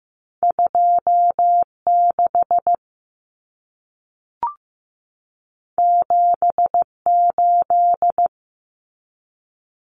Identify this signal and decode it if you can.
Morse code, heard as a keyed tone, about 15 wpm, 700 Hz. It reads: 26 78
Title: Common Cut Numbers - 2-Digit 15wpm